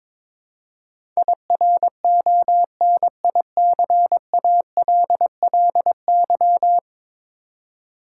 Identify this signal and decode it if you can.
Morse code, heard as a keyed tone, about 22 wpm, 700 Hz. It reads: IRONICALLY
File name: practice / 22wpm / 10 Letter Words - First 3 Letters Given - 22wpm